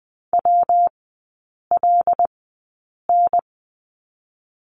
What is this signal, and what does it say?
Morse code, heard as a keyed tone, about 20 wpm, 700 Hz. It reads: W L N
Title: 114 Lesson 19e - Letter w - Three Characters 20wpm